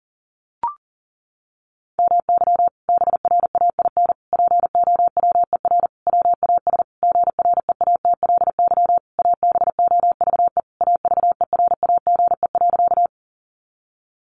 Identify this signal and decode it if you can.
Morse code, heard as a keyed tone, about 40 wpm, 700 Hz. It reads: MY BRAIN POWER WAS GREATLY ABOVE AVERAGE.
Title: Sentences from Top 1400 Words 40wpm